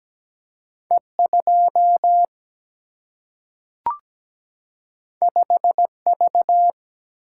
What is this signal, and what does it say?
Morse code, heard as a keyed tone, about 17 wpm, 700 Hz. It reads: E 2 5 V
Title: Single-Letter Single-Number 17wpm